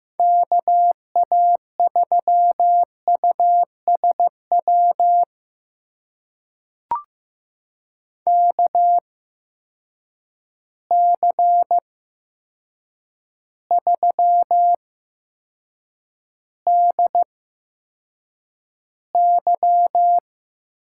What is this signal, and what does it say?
Morse code, heard as a keyed tone, about 15 wpm, spaced as for 7 wpm, 700 Hz. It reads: KA3USW KC3DY
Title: USA Call Signs - 6 Characters - 8x Character Spacing - 15wpm